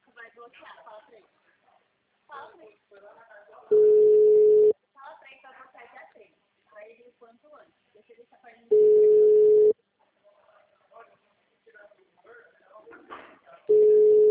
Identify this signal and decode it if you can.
Ringback tone (Europe, 425 Hz), with speech.
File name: out- -2011